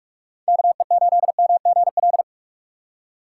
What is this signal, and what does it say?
Morse code, heard as a keyed tone, about 44 wpm, 700 Hz. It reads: KE8MGL
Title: USA Call Signs - 6 Characters - 6x Character Spacing - 45wpm